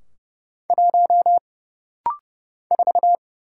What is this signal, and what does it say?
Morse code, heard as a keyed tone, about 30 wpm, 700 Hz. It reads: 1 4
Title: Single Letter-Number Alt - Rapid-Fire 30wpm